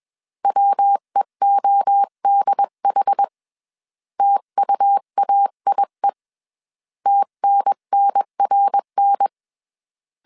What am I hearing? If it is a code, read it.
Morse code, heard as a keyed tone, about 21 wpm, 800 Hz. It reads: WEODH TUAIE TNNRN
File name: CW / DET / 1 3 1500C S B 20wpm 1754